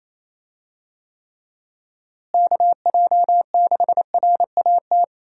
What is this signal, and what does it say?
Morse code, heard as a keyed tone, about 28 wpm, 700 Hz. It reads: KJ6RAT
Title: USA Call Signs - 6 Characters - Repeated 3x - 28wpm